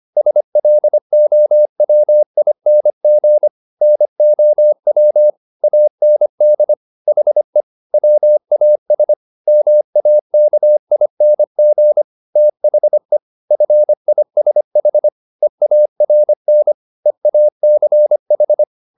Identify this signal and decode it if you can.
Morse code, heard as a keyed tone, about 25 wpm, 600 Hz. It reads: SLOWING NOW AND HE WAS MAKING THE FISH EARN EACH